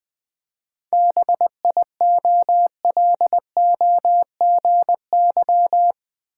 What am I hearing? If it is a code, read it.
Morse code, heard as a keyed tone, about 20 wpm, 700 Hz. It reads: BIOLOGY
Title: 7 Letter Words - First 3 Letters Given - 20wpm